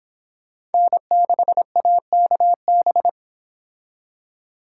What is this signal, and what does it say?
Morse code, heard as a keyed tone, about 31 wpm, 700 Hz. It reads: N6AKB